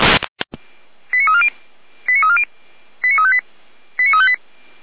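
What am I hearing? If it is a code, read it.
Five-tone selective call (ZVEI), heard as quick runs of five tones, tones of about 70 ms. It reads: 89340 89340 89358 89358
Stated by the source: ZVEI five-tone call received on an SDR, from the Feuerwehr Orlamünde fire brigade audio server.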